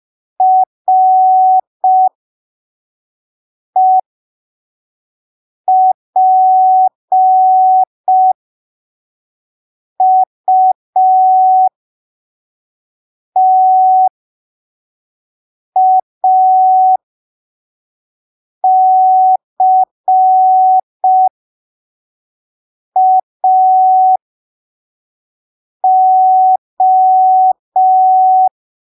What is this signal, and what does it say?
Morse code, heard as a keyed tone, about 5 wpm, spaced as for 3 wpm, 750 Hz. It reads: REPUTACAO